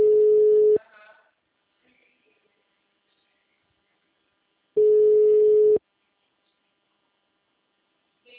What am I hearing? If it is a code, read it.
Ringback tone (Europe, 425 Hz), with speech.